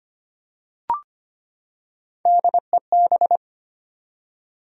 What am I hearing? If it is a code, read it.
Morse code, heard as a keyed tone, about 25 wpm, 700 Hz. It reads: DEB
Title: USA Common Names - Max 5 Letters - 25wpm